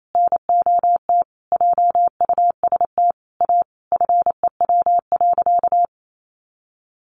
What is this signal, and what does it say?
Morse code, heard as a keyed tone, about 28 wpm, 700 Hz. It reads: NOT JUST A FEW.
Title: Sentences from Top 1500 Words 28wpm